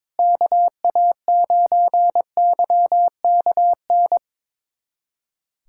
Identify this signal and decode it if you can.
Morse code, heard as a keyed tone, about 22 wpm, 700 Hz. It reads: KA9YKN
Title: USA Callsign DE Callsign - 6 Characters - 22wpm